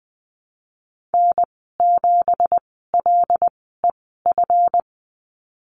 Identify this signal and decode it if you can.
Morse code, heard as a keyed tone, about 20 wpm, spaced as for 15 wpm, 700 Hz. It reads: N7LEF
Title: 219 Lesson 35g - Number 0 - Call Signs 20wpm